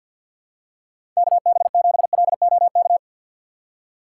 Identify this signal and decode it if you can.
Morse code, heard as a keyed tone, about 49 wpm, 700 Hz. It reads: KB7LOK
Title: USA Call Signs - 6 Characters - 3x Character Spacing - 50wpm